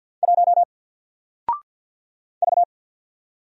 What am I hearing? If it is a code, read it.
Morse code, heard as a keyed tone, about 49 wpm, 700 Hz. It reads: 1 V
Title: Single Letter-Number 50wpm